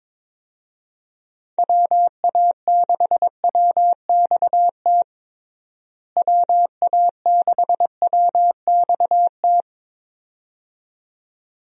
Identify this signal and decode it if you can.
Morse code, heard as a keyed tone, about 22 wpm, 700 Hz. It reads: WA6WXT WA6WXT
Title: USA Call Signs - 6 Characters - Repeated 2x - 22wpm